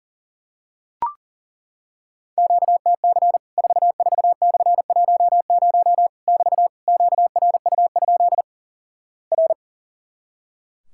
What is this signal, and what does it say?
Morse code, heard as a keyed tone, about 37 wpm, 700 Hz. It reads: QTC 44/10 <BT> QRU? R
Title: Worked All Europe DX Contest - QTC Exchange - 40wpm